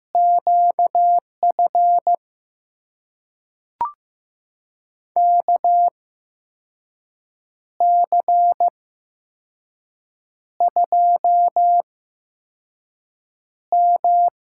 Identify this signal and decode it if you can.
Morse code, heard as a keyed tone, about 15 wpm, spaced as for 5 wpm, 700 Hz. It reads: QF KC2M
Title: USA Call Signs - 6 Characters - 8x Character Spacing - 15wpm